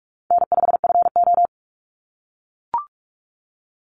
Morse code, heard as a keyed tone, about 44 wpm, 700 Hz. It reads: N5FO
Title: Call Signs - 45wpm